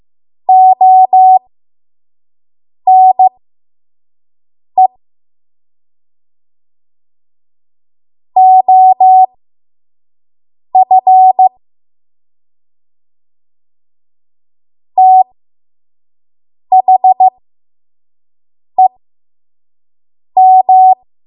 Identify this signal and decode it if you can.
Morse code, heard as a keyed tone, about 15 wpm, spaced as for 5 wpm, 750 Hz. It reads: ONE OF THEM
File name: Morse / 05WPM